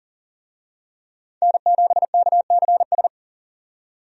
Morse code, heard as a keyed tone, about 40 wpm, 700 Hz. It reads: N7KCS
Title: USA Call Signs - 5 Characters - 40wpm